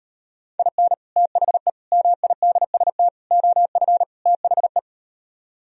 Morse code, heard as a keyed tone, about 39 wpm, 700 Hz. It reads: IN THE MIDST OF THE